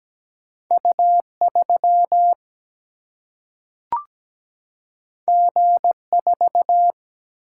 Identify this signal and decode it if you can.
Morse code, heard as a keyed tone, about 17 wpm, 700 Hz. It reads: U3 G4
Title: Single-Letter Single-Number 17wpm